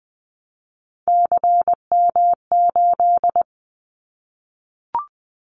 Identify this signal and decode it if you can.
Morse code, heard as a keyed tone, about 20 wpm, 700 Hz. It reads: CM8
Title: Callsign Prefixes for Top 100 DXCC - 20wpm